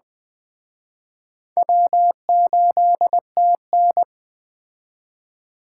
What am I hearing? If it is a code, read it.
Morse code, heard as a keyed tone, about 20 wpm, 700 Hz. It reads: W8TN